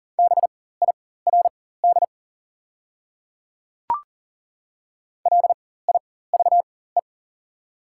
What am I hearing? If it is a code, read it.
Morse code, heard as a keyed tone, about 40 wpm, spaced as for 19 wpm, 690 Hz. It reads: BIRD LIVE
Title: 4x Character Spacing - Top 500 Words - 40wpm